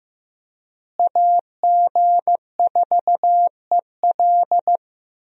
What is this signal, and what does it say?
Morse code, heard as a keyed tone, about 15 wpm, 700 Hz. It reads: AG4EL
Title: USA Call Signs - 5 Characters - Repeated 3x - 15wpm